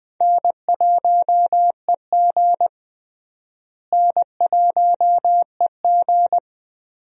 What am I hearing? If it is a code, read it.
Morse code, heard as a keyed tone, about 21 wpm, 700 Hz. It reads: N1EG N1EG
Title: USA Call Signs - 4 Characters - Repeated 3x - 20wpm